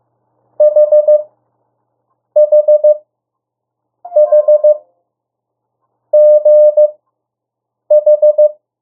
Morse code, heard as a keyed tone, about 15 wpm, spaced as for 6 wpm, 600 Hz. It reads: HHHGH
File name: urok 1